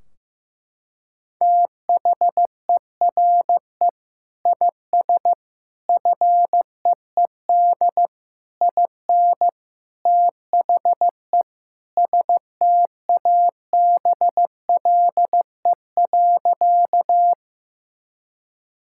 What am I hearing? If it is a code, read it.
Morse code, heard as a keyed tone, about 16 wpm, 700 Hz. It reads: THERE IS FEED IN THE STABLE.
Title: Sentences from Top 1600 Words 15wpm